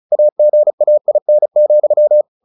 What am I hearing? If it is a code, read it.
Morse code, heard as a keyed tone, about 34 wpm, 600 Hz. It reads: AGAIN,
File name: the old man and the sea 2 35wpm